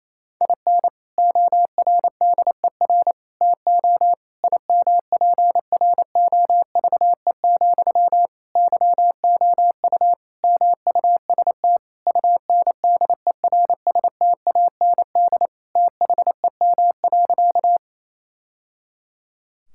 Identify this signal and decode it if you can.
Morse code, heard as a keyed tone, about 28 wpm, 700 Hz. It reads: IN ORDER TO IMPROVE, YOU MUST UNDERSTAND THEM.
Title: Sentences from Top 1000 Words - Review Part 5 - 28wpm